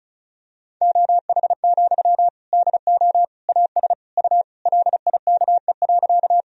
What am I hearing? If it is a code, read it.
Morse code, heard as a keyed tone, about 36 wpm, 700 Hz. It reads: OH, DO AS U LIKE.
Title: Sentences from Top 500 Words and Abbreviations 35wpm